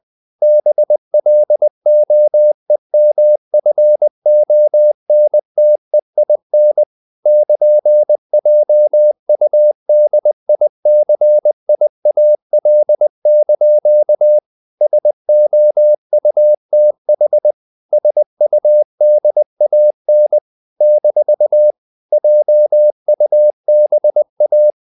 Morse code, heard as a keyed tone, about 20 wpm, 600 Hz. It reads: BLOEMFONTEIN <KN>JUDICIAL) SOUTH SUDAN - JUBA
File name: CW / Practice / NationalCapitals20